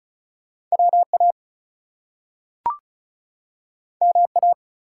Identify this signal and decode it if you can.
Morse code, heard as a keyed tone, about 35 wpm, 700 Hz. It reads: WA MA
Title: US States and Canadian Provinces 35wpm